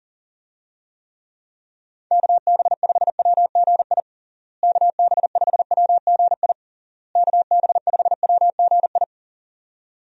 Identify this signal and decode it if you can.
Morse code, heard as a keyed tone, about 40 wpm, 700 Hz. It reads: KB5WGI KB5WGI KB5WGI
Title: USA Call Signs - 6 Characters - Repeated 3x - 40wpm